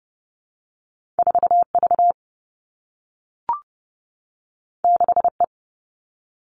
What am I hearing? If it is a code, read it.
Morse code, heard as a keyed tone, about 30 wpm, 700 Hz. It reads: 4 V 6 E
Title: Two Challenging Letter-Number Combinations 30wpm